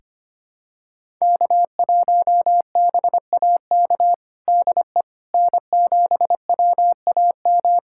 Morse code, heard as a keyed tone, about 25 wpm, 700 Hz. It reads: K1BAK DE N7WAM
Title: USA Callsign DE Callsign - 5 Characters - 25wpm